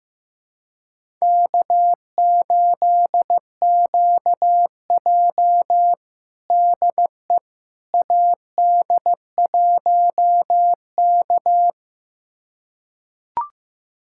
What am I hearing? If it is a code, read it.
Morse code, heard as a keyed tone, about 15 wpm, 700 Hz. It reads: K8QJ DE AD1K
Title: USA Callsign DE Callsign - 4 Characters - 15wpm